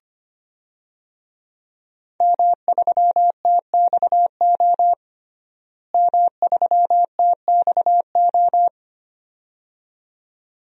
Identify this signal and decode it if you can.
Morse code, heard as a keyed tone, about 25 wpm, 700 Hz. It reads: M3TXO M3TXO
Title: DXCC-100 Call Signs - 5 Characters - Repeated 2x - 25wpm